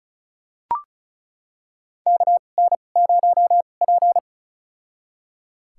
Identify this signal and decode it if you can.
Morse code, heard as a keyed tone, about 36 wpm, spaced as for 25 wpm, 700 Hz. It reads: KN0P
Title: USA Call Signs - 4 Characters - 2x Character Spacing - 35wpm